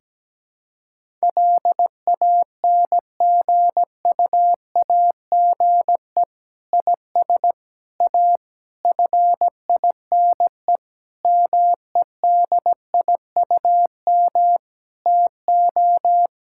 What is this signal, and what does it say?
Morse code, heard as a keyed tone, about 17 wpm, 700 Hz. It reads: LANGUAGE IS A FINE MEDIUM TO
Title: Sentences from Top 1000 Words - Review Part 6 - 17wpm